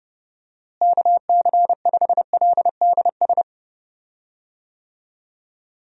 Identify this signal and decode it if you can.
Morse code, heard as a keyed tone, about 32 wpm, 700 Hz. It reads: KC5LDS
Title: USA Call Signs - 6 Characters - Repeated 3x - 30wpm